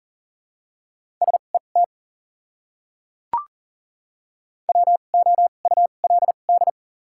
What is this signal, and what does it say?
Morse code, heard as a keyed tone, about 40 wpm, spaced as for 29 wpm, 700 Hz. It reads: SET WOULD
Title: 2x Character Spacing - Top 300 Words - 40wpm